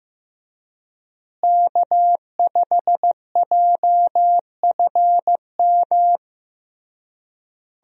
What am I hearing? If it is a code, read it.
Morse code, heard as a keyed tone, about 16 wpm, 700 Hz. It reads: K5JFM
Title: USA Call Signs - 5 Characters - Repeated 3x - 15wpm